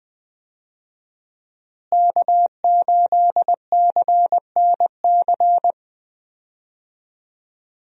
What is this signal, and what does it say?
Morse code, heard as a keyed tone, about 19 wpm, 700 Hz. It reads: K8CNC
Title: USA Call Signs - 5 Characters - Repeated 3x - 20wpm